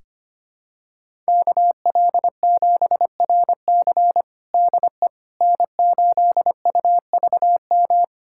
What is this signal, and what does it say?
Morse code, heard as a keyed tone, about 25 wpm, 700 Hz. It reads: KL7RC DE N8UVM
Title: USA Callsign DE Callsign - 5 Characters - 25wpm